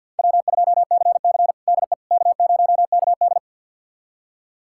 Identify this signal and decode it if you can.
Morse code, heard as a keyed tone, about 50 wpm, 700 Hz. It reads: W2CC DE K0KD
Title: USA Callsign DE Callsign - 4 Characters - 50wpm